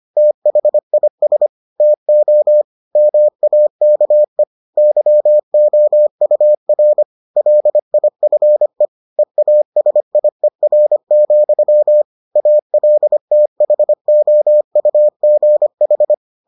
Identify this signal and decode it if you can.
Morse code, heard as a keyed tone, about 25 wpm, 600 Hz. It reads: THIS TO MAKE YOUR LIFE EASIER, ALTHOUGH